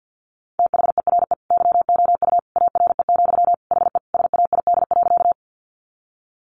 Morse code, heard as a keyed tone, about 50 wpm, 700 Hz. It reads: THERE YOU ARE, HE SAID.
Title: Sentences from Top 100 Words 50wpm